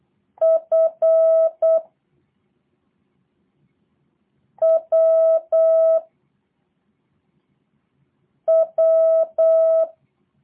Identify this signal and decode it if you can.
Morse code, heard as a keyed tone, about 8 wpm, 650 Hz. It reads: F W W
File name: jh8cbh / 2023cw / review1-4